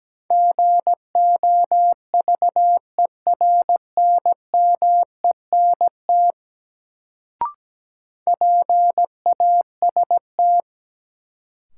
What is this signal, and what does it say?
Morse code, heard as a keyed tone, about 17 wpm, 700 Hz. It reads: GOVERNMENT PAST
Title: Sentences from Top 600 Words - New Word Drill - 17wpm